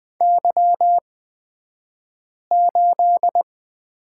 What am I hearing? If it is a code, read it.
Morse code, heard as a keyed tone, about 20 wpm, spaced as for 5 wpm, 700 Hz. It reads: Y8
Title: 223 Lesson 36d - BT - Two Characters 20wpm